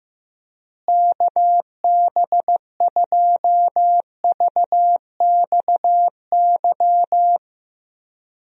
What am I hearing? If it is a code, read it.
Morse code, heard as a keyed tone, about 15 wpm, 700 Hz. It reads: KB2VXY
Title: USA Call Signs - 6 Characters - Repeated 3x - 15wpm